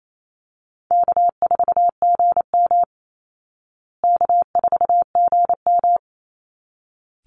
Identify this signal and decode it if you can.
Morse code, heard as a keyed tone, about 28 wpm, 700 Hz. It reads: K4GM K4GM
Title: Easy Call Signs Repeated 2x - 28wpm